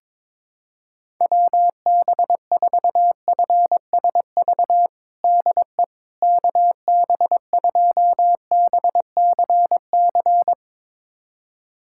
Morse code, heard as a keyed tone, about 22 wpm, 700 Hz. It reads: WB4FSV DE KB2BCC
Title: USA Callsign DE Callsign - 6 Characters - 22wpm